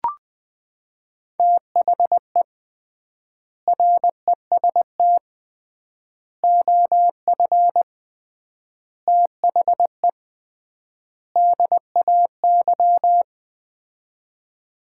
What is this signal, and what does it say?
Morse code, heard as a keyed tone, about 20 wpm, 700 Hz. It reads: THE REST OF THE DAY
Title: 3x Word-Spacing - Sets of 5 Words - 20wpm